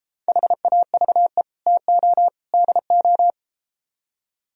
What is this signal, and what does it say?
Morse code, heard as a keyed tone, about 33 wpm, 700 Hz. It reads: HAVE TO DO